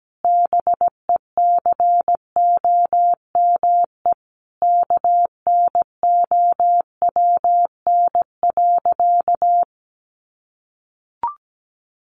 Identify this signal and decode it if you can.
Morse code, heard as a keyed tone, about 17 wpm, 700 Hz. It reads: BECOME KNOWN.
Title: Sentences from Top 1300 Words 17wpm